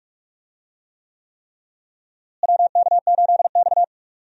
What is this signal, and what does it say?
Morse code, heard as a keyed tone, about 46 wpm, 700 Hz. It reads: WK8X
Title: USA Call Signs - 4 Characters - Repeated 3x - 45wpm